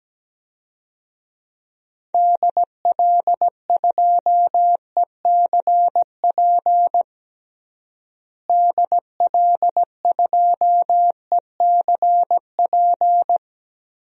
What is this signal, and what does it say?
Morse code, heard as a keyed tone, about 17 wpm, 700 Hz. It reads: DL2ECP DL2ECP
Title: DXCC-100 Call Signs - 6 Characters - Repeated 3x - 17wpm